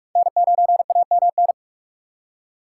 Morse code, heard as a keyed tone, about 46 wpm, 690 Hz. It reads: N9AMN
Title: USA Call Signs - 5 Characters - 6x Character Spacing - 45wpm